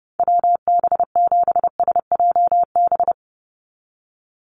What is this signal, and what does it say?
Morse code, heard as a keyed tone, about 30 wpm, 700 Hz. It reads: WB7SJB
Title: USA Call Signs 30wpm